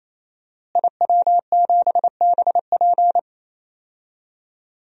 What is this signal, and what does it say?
Morse code, heard as a keyed tone, about 28 wpm, 700 Hz. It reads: IW7BP
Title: DXCC-100 Call Signs - 5 Characters - Repeated 3x - 28wpm